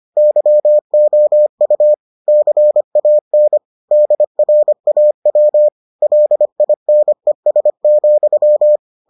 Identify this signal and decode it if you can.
Morse code, heard as a keyed tone, about 25 wpm, 600 Hz. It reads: YOU CAN DRAW LINES,